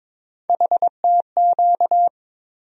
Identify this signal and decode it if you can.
Morse code, heard as a keyed tone, about 22 wpm, 700 Hz. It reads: HTQ
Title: USA Call Sign Postfixes - 3 Letters 22wpm